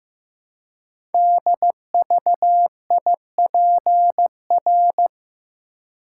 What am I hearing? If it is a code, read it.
Morse code, heard as a keyed tone, about 15 wpm, 700 Hz. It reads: DVIPR
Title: Code Groups - 5 Letters - 15wpm